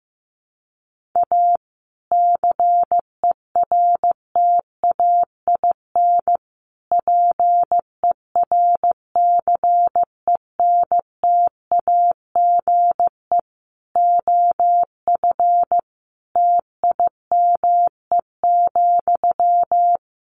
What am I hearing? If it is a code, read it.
Morse code, heard as a keyed tone, about 15 wpm, 700 Hz. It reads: A CERTAIN PERCENTAGE OF TIME,
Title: Sentences from Top 1300 Words 15wpm